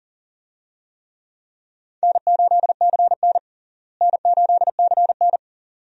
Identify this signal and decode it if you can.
Morse code, heard as a keyed tone, about 40 wpm, 700 Hz. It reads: N8CN N8CN
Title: USA Call Signs - 4 Characters - Repeated 2x - 40wpm